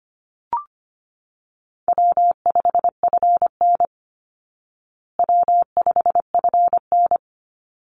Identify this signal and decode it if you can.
Morse code, heard as a keyed tone, about 25 wpm, 700 Hz. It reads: W5FN W5FN
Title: Call Signs Repeated 2x - 25wpm